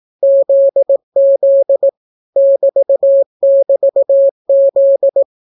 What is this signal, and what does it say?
Morse code, heard as a keyed tone, about 18 wpm, 550 Hz. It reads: ZZ <BT><BT>Z